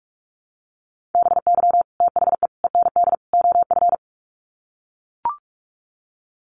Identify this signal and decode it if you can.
Morse code, heard as a keyed tone, about 44 wpm, 700 Hz. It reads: BY THE END OF
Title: Sets of 4 Words 45wpm